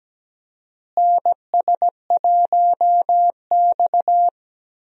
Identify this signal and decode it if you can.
Morse code, heard as a keyed tone, about 17 wpm, 700 Hz. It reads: NS1X
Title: USA Call Signs - 4 Characters - Repeated 2x - 17wpm